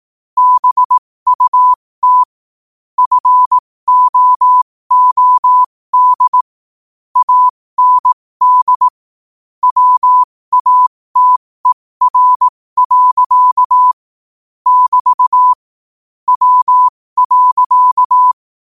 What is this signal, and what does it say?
Morse code, heard as a keyed tone, about 18 wpm, spaced as for 15 wpm, 1000 Hz. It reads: BUT FOOD AND WATER. <BT> W.
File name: qotdfiles / 15wpm